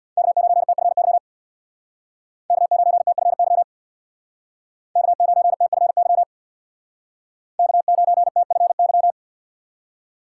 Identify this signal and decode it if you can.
Morse code, heard as a keyed tone, about 63 wpm, 700 Hz. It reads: K9TFY K9TFY K9TFY K9TFY